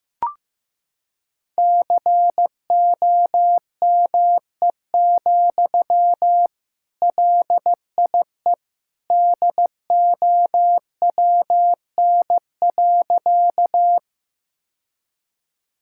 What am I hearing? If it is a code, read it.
Morse code, heard as a keyed tone, about 15 wpm, 700 Hz. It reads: COME, LIE DOWN.
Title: Sentences from Top 1600 Words 15wpm